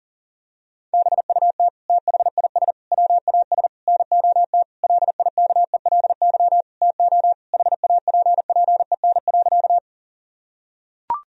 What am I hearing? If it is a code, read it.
Morse code, heard as a keyed tone, about 40 wpm, 700 Hz. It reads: BUT THIS WAS NOT LIKELY TO HAPPEN.
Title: Sentences from Top 1000 Words - Review Part 4 - 40wpm